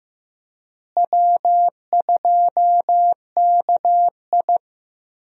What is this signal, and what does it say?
Morse code, heard as a keyed tone, about 15 wpm, 700 Hz. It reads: W2KI
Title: USA Call Signs - 4 Characters - Repeated 2x - 15wpm